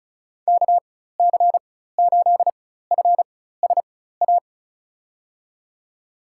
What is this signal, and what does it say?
Morse code, heard as a keyed tone, about 34 wpm, spaced as for 16 wpm, 700 Hz. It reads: KC8FSA
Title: USA Call Signs - 4x Character Spacing - 35wpm